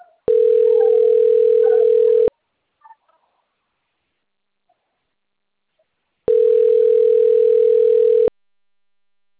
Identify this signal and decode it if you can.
Ringback tone (North America), 440 + 480 Hz.
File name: rg-1600-unknown- - - .63714